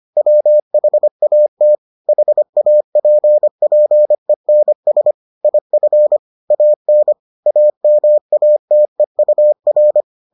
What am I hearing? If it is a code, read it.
Morse code, heard as a keyed tone, about 25 wpm, 600 Hz. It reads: WHAT HAPPENS IF AN AMATEUR